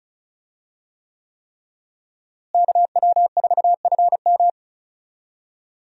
Morse code, heard as a keyed tone, about 34 wpm, 700 Hz. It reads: KW4FM
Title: USA Call Signs - 5 Characters - Repeated 3x - 35wpm